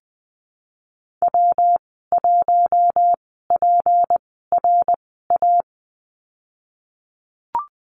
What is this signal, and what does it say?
Morse code, heard as a keyed tone, about 20 wpm, spaced as for 15 wpm, 700 Hz. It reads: W1PRA
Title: 154 Lesson 25g - Letter p - Call Signs 20wpm